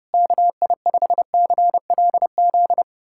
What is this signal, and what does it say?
Morse code, heard as a keyed tone, about 30 wpm, 700 Hz. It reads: KI5CLZ
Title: USA Callsign DE Callsign - 6 Characters - 30wpm